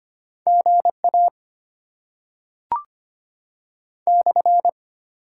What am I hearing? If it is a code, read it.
Morse code, heard as a keyed tone, about 25 wpm, 690 Hz. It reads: GA /
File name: Top 90 QSO Elements - 8x Character Spacing - 25wpm